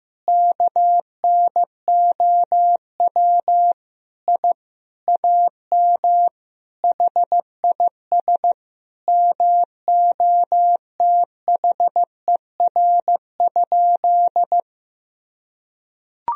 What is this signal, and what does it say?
Morse code, heard as a keyed tone, about 15 wpm, 700 Hz. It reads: KNOW I AM HIS MOTHER?
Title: Sentences from Top 1000 Words - Review Part 2 - 15wpm